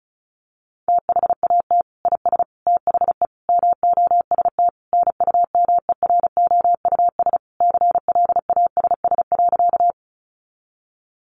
Morse code, heard as a keyed tone, about 34 wpm, 700 Hz. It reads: THAT IS THE MOST NUMEROUS CLASS.
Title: Sentences from Top 1400 Words 35wpm